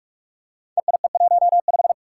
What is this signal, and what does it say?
Morse code, heard as a keyed tone, about 44 wpm, 700 Hz. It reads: EIE15
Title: Five Challenging Letter-Number Combinations 45wpm